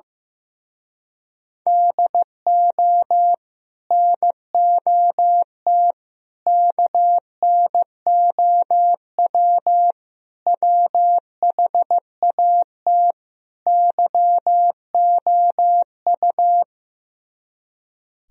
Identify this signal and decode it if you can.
Morse code, heard as a keyed tone, about 15 wpm, 700 Hz. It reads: DO NOT KNOW WHAT YOU
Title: Speed-Racing - Sets of 5 Words 15wpm